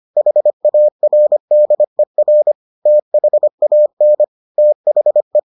Morse code, heard as a keyed tone, about 25 wpm, 600 Hz. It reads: HARDER THAN THE